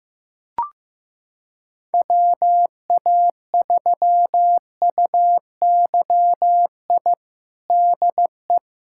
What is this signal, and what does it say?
Morse code, heard as a keyed tone, about 15 wpm, 700 Hz. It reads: WA3UYI DE
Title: USA Callsign DE Callsign - 6 Characters - 15wpm